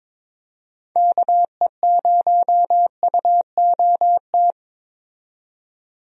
Morse code, heard as a keyed tone, about 22 wpm, 700 Hz. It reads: KE0UOT
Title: USA Call Signs - 6 Characters - 22wpm